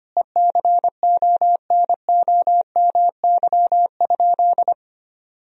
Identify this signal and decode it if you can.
Morse code, heard as a keyed tone, about 25 wpm, 700 Hz. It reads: ECONOMY?
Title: Sentences from Top 1000 Words - Review Part 4 - 25wpm